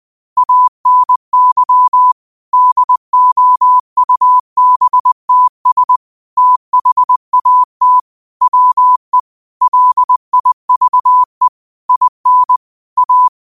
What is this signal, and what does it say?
Morse code, heard as a keyed tone, about 20 wpm, 1000 Hz. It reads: ANY DOUBTS THAT WE LIVE IN A